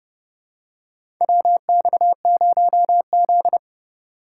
Morse code, heard as a keyed tone, about 31 wpm, 700 Hz. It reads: WX0Z
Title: K1USN SST QSO Practice - 30wpm